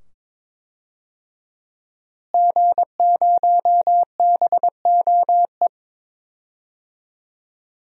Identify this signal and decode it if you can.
Morse code, heard as a keyed tone, about 22 wpm, 700 Hz. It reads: G0BOE
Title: DXCC-100 Call Signs - 5 Characters - Repeated 2x - 22wpm